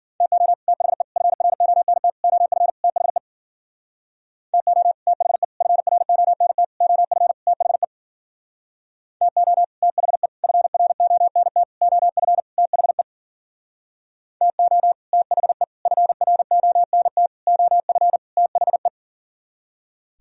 Morse code, heard as a keyed tone, about 52 wpm, 700 Hz. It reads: TO THE FRONT OF THE TO THE FRONT OF THE TO THE FRONT OF THE TO THE FRONT OF THE